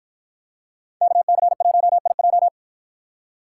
Morse code, heard as a keyed tone, about 53 wpm, 700 Hz. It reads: KC1IJ